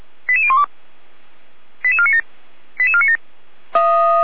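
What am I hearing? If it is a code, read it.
Five-tone selective call (ZVEI), heard as quick runs of five tones, tones of about 70 ms. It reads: 88012 88497 88497
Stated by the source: ZVEI five-tone selective call received by SDR.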